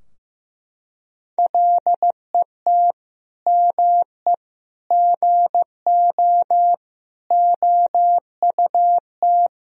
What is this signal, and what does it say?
Morse code, heard as a keyed tone, about 16 wpm, 700 Hz. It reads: LET ME GO OUT
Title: Sentences from Top 1000 Words - Review Part 4 - 15wpm